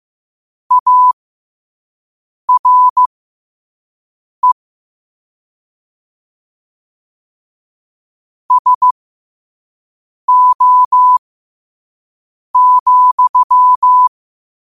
Morse code, heard as a keyed tone, about 15 wpm, spaced as for 5 wpm, 1000 Hz. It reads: ARE SO,
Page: New quotes every day in morse code at 5 Words per minute.